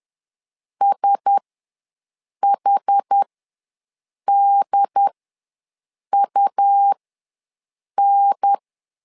Morse code, heard as a keyed tone, about 10 wpm, spaced as for 6 wpm, 800 Hz. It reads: SHDUN
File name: LEZIONI / 4 Q1S ws 10wpm 2208